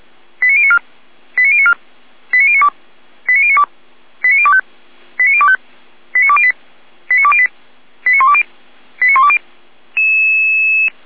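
Five-tone selective call (ZVEI), heard as quick runs of five tones, tones of about 70 ms. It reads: 89094 89094 89092 89092 89025 89025 89208 89208 89120 89120
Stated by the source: ZVEI five-tone selective call received on an SDR.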